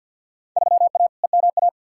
Morse code, heard as a keyed tone, about 53 wpm, 700 Hz. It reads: 3A EMA
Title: Field Day Exchanges - 50wpm